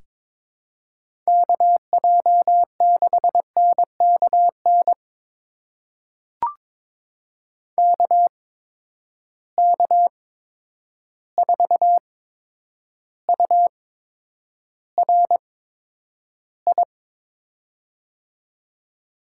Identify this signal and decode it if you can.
Morse code, heard as a keyed tone, about 22 wpm, spaced as for 9 wpm, 700 Hz. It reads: KJ6NKN KK4URI
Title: USA Call Signs - 6 Characters - 8x Character Spacing - 22wpm